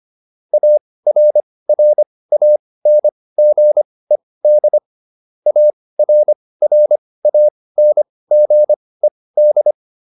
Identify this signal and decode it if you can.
Morse code, heard as a keyed tone, about 25 wpm, spaced as for 18 wpm, 600 Hz. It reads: ARRANGED ARRANGED